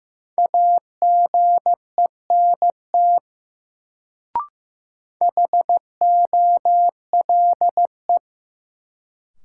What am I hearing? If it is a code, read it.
Morse code, heard as a keyed tone, about 15 wpm, 700 Hz. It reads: AGENT HOLE
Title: Sentences from Top 1600 Words - New Word Drill Alt - 15wpm